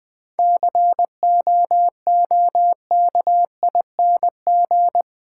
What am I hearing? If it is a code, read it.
Morse code, heard as a keyed tone, about 20 wpm, 700 Hz. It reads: COOKING
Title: 7 Letter Words - First 3 Letters Given - 20wpm